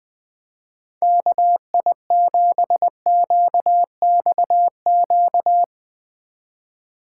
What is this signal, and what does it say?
Morse code, heard as a keyed tone, about 19 wpm, 700 Hz. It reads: KI7QXQ
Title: Speed Racing - USA 2x3 Call Signs 17wpm